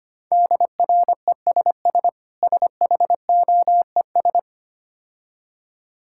Morse code, heard as a keyed tone, about 26 wpm, 700 Hz. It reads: DRESS SHOES
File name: Easy Words - Speed-Racing with Hints - 25wpm